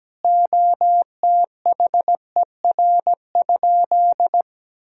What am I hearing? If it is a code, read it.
Morse code, heard as a keyed tone, about 17 wpm, 700 Hz. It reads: OTHER?
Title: Sentences from Top 1000 Words - Review Part 3 - 17wpm